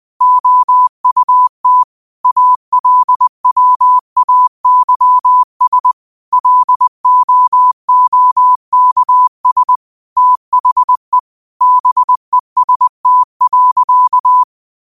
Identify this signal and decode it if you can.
Morse code, heard as a keyed tone, about 20 wpm, 1000 Hz. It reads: OUT ALWAYS LOOKS THE BEST.
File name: qotdfiles / 20wpm